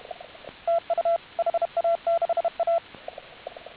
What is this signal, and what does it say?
Morse code, heard as a keyed tone, about 32 wpm, 1350 Hz. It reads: TU HA6A